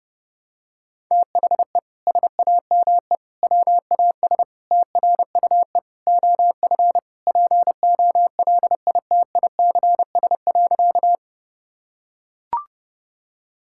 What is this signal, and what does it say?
Morse code, heard as a keyed tone, about 32 wpm, 700 Hz. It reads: THE SAME WAS TRUE OF POLITICS.
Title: Sentences from Top 1000 Words - Review Part 6 - 30wpm